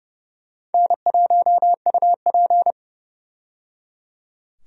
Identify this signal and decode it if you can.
Morse code, heard as a keyed tone, about 30 wpm, 700 Hz. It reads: N1UP
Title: USA Call Signs - 4 Characters - 30wpm